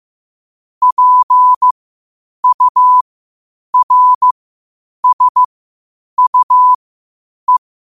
Morse code, heard as a keyed tone, about 15 wpm, spaced as for 8 wpm, 1000 Hz. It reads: PURSUE
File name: qotdfiles / 8wpm